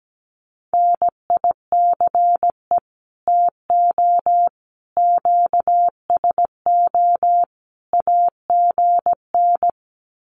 Morse code, heard as a keyed tone, about 17 wpm, 700 Hz. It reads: NICE TO QSO AGN
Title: Common QSO Phrases 17wpm